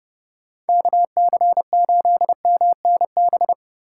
Morse code, heard as a keyed tone, about 27 wpm, 700 Hz. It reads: KC8MNB